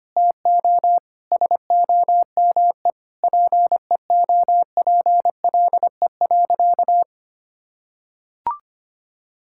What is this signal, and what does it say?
Morse code, heard as a keyed tone, about 25 wpm, 700 Hz. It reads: TO SOME PEOPLE.
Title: Sentences from Top 1000 Words - Review Part 5 - 25wpm